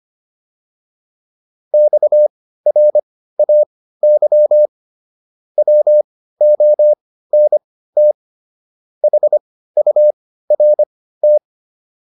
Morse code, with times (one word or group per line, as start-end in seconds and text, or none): 1.7-4.7 XRAY
5.6-8.1 WONT
9.0-11.4 HURT